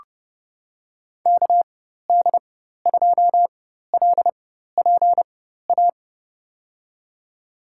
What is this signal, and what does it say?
Morse code, heard as a keyed tone, about 31 wpm, spaced as for 14 wpm, 700 Hz. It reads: KD2LPA
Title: USA Call Signs - 6 Characters - 4x Character Spacing - 30wpm